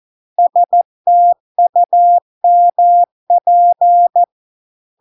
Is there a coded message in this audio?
Yes: Morse code from 0.4 to 4.2 s, heard as a keyed tone, about 14 wpm, 700 Hz. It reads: STUMP